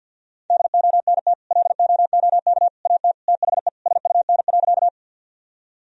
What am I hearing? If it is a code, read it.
Morse code, heard as a keyed tone, about 50 wpm, 700 Hz. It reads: DONT LOOK AT THE SUN.
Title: Sentences from Top 1000 Words - Review Part 1 - 50wpm